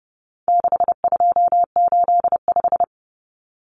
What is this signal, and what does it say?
Morse code, heard as a keyed tone, about 30 wpm, 700 Hz. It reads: 6285
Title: Common Cut Numbers - 4-Digit 30wpm